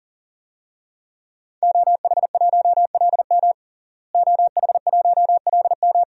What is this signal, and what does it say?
Morse code, heard as a keyed tone, about 40 wpm, 700 Hz. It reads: OH1LM OH1LM
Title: DXCC-100 Call Signs - 5 Characters - Repeated 2x - 40wpm